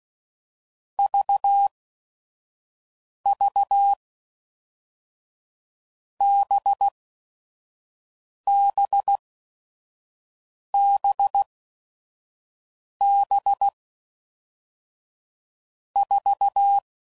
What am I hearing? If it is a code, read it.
Morse code, heard as a keyed tone, about 16 wpm, spaced as for 5 wpm, 800 Hz. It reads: VV BBBB 4
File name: Groups / Basic Letter Groups / VB46